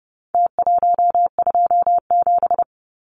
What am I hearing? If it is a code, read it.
Morse code, heard as a keyed tone, about 30 wpm, 700 Hz. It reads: T127
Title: Numbers - Contest Style - 4-Digit 30wpm